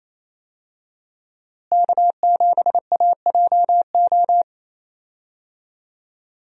Morse code, heard as a keyed tone, about 28 wpm, 700 Hz. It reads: K7AJO
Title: USA Call Signs - 5 Characters - Repeated 3x - 28wpm